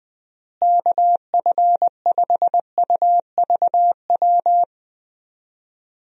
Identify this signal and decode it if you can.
Morse code, heard as a keyed tone, about 20 wpm, 700 Hz. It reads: KF5UVW
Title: USA Call Signs - 6 Characters - Repeated 2x - 20wpm